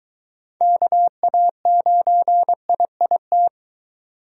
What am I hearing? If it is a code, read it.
Morse code, heard as a keyed tone, about 23 wpm, 700 Hz. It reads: KA9IIT